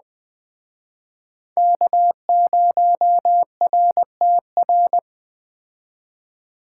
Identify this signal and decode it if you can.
Morse code, heard as a keyed tone, about 21 wpm, 700 Hz. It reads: K0RTR